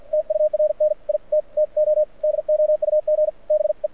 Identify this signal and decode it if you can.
Morse code, heard as a keyed tone, about 54 wpm, spaced as for 46 wpm, 600 Hz. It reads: TURN IT TO DOUG BE
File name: archived / CRSnet